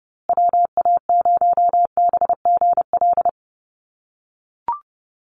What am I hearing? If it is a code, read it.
Morse code, heard as a keyed tone, about 30 wpm, 700 Hz. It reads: WA0BGL
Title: USA Call Signs Repeated 3x 30wpm